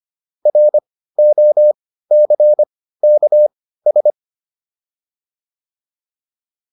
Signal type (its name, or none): Morse code